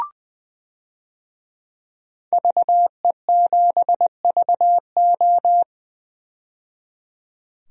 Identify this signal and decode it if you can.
Morse code, heard as a keyed tone, about 20 wpm, 700 Hz. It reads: VE7VO
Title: DXCC-100 Call Signs - 5 Characters - 20wpm